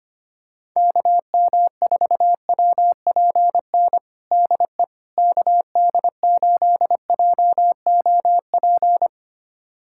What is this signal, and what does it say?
Morse code, heard as a keyed tone, about 25 wpm, 700 Hz. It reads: KM4WPN DE KD8JOP
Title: USA Callsign DE Callsign - 6 Characters - 25wpm